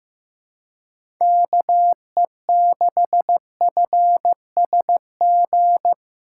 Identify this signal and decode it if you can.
Morse code, heard as a keyed tone, about 15 wpm, 700 Hz. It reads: KE6FSG
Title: USA Call Signs - 6 Characters - 3x Character Spacing - 15wpm